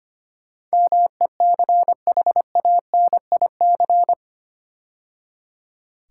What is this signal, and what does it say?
Morse code, heard as a keyed tone, about 25 wpm, 700 Hz. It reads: MECHANIC
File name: 8 Letter Words - First 3 Letters Given - 25wpm